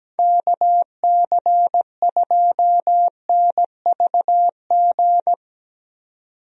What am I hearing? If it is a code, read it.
Morse code, heard as a keyed tone, about 17 wpm, 700 Hz. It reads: KC2NVG
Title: USA Call Signs - 6 Characters - 17wpm